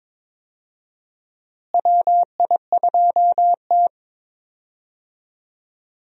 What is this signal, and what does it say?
Morse code, heard as a keyed tone, about 22 wpm, 700 Hz. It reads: WI2T
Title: USA Call Signs - 4 Characters - Repeated 2x - 22wpm